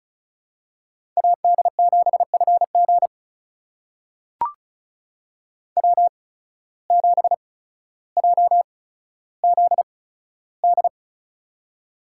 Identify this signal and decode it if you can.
Morse code, heard as a keyed tone, about 36 wpm, spaced as for 15 wpm, 700 Hz. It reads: AD7FG W7JZD
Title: USA Call Signs - 5 Characters - 8x Character Spacing - 35wpm